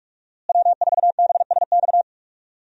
Morse code, heard as a keyed tone, about 41 wpm, 700 Hz. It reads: W4BSX
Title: USA Call Signs - 5 Characters - Repeated 2x - 45wpm